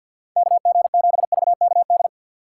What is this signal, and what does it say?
Morse code, heard as a keyed tone, about 53 wpm, 690 Hz. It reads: KG7VKD
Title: USA Call Signs - 2x Character Spacing - 50wpm